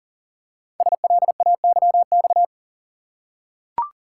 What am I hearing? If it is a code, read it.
Morse code, heard as a keyed tone, about 38 wpm, 700 Hz. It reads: SLAYX
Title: Code Groups - 5 Letters - 40wpm